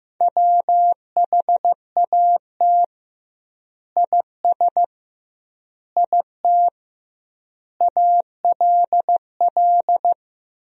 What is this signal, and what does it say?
Morse code, heard as a keyed tone, about 15 wpm, 700 Hz. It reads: WHAT IS IT ALL
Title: 2x Word-Spacing - Sentences from Top 100 Words - 15wpm